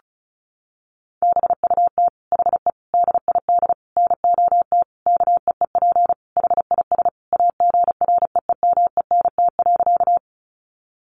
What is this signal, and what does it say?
Morse code, heard as a keyed tone, about 34 wpm, 700 Hz. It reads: BUT HE DID NOT KEEP HIS AGREEMENT.
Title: Sentences from Top 1500 Words 35wpm